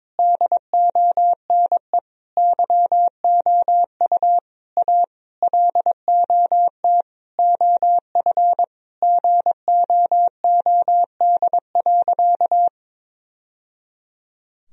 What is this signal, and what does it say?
Morse code, heard as a keyed tone, about 22 wpm, 700 Hz. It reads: DONE YOU A LOT OF GOOD.
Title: Sentences from Top 1000 Words - Review Part 3 - 22wpm